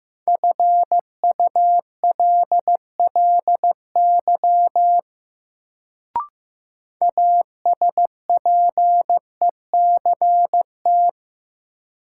Morse code, heard as a keyed tone, about 15 wpm, 700 Hz. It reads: FULLY ASPECT
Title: Sentences from Top 1300 Words - New Word Drill - 15wpm